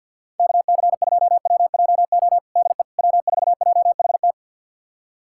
Morse code, heard as a keyed tone, about 46 wpm, 700 Hz. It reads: KC2WJO DE W4JHT
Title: USA Callsign DE Callsign 50wpm